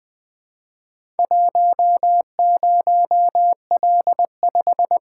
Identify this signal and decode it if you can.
Morse code, heard as a keyed tone, about 20 wpm, 700 Hz. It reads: 10L5
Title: Four Letter-Number Combinations - 20wpm